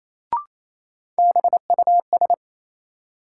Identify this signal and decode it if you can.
Morse code, heard as a keyed tone, about 28 wpm, 700 Hz. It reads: BUS
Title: Sentences from Top 1000 Words - Review Part 6 - New Word Drill - 28wpm